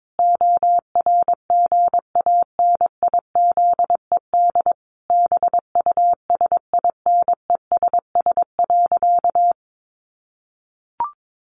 Morse code, heard as a keyed tone, about 22 wpm, 700 Hz. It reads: ORGANIZED BUSINESS.
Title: Sentences from Top 1300 Words 22wpm